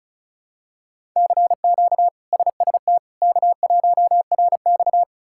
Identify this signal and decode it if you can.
Morse code, heard as a keyed tone, about 36 wpm, 700 Hz. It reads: CQ SST K1RX
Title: K1USN SST QSO Practice - 35wpm